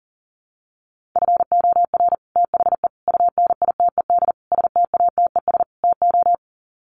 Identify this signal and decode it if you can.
Morse code, heard as a keyed tone, about 40 wpm, 700 Hz. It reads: FOR THE UNITED STATES TO
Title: Sets of 5 Words Encore 40wpm